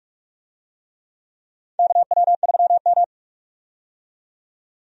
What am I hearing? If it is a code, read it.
Morse code, heard as a keyed tone, about 47 wpm, 700 Hz. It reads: KW3M
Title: USA Call Signs - 4 Characters - 45wpm